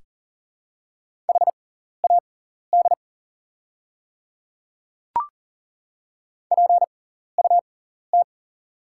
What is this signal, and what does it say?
Morse code, heard as a keyed tone, about 40 wpm, spaced as for 14 wpm, 700 Hz. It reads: HAD PUT
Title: 6x Character Spacing - Top 300 Words - 40wpm